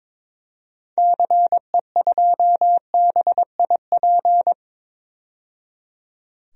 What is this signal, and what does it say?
Morse code, heard as a keyed tone, about 22 wpm, 700 Hz. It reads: CE2BIP
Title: DXCC-100 Call Signs - 6 Characters - 22wpm